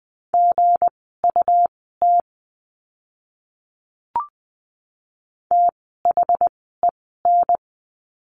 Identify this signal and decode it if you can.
Morse code, heard as a keyed tone, about 20 wpm, spaced as for 14 wpm, 700 Hz. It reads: GUT THEN
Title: 160 Lesson 26f - Letter g - Words 20wpm